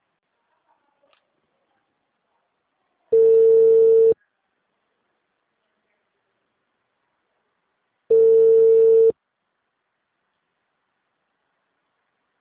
Ringback tone (China, 450 Hz).